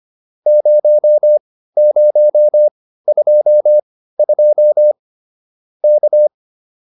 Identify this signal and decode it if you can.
Morse code, heard as a keyed tone, about 25 wpm, spaced as for 15 wpm, 590 Hz. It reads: 0022 K